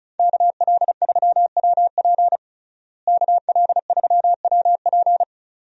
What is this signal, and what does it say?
Morse code, heard as a keyed tone, about 36 wpm, 700 Hz. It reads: KL3WP KL3WP
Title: USA Call Signs - 5 Characters - Repeated 2x - 35wpm